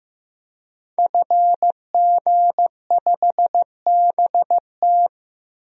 Morse code, heard as a keyed tone, about 15 wpm, 700 Hz. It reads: FG5BT
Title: DXCC-100 Call Signs - 5 Characters - 15wpm